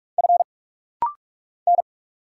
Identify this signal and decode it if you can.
Morse code, heard as a keyed tone, about 44 wpm, 700 Hz. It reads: F N
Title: Single Letter-Number Alt - Rapid-Fire 45wpm